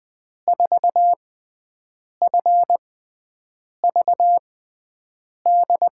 Morse code, heard as a keyed tone, about 21 wpm, spaced as for 7 wpm, 700 Hz. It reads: 4FVD
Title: USA Call Signs - 6x Character Spacing - 20wpm